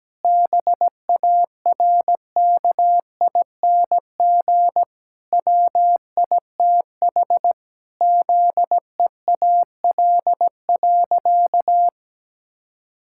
Morse code, heard as a keyed tone, about 17 wpm, 700 Hz. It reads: BARKING WITH ZEAL.
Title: Pangram Challenge v2 - 17wpm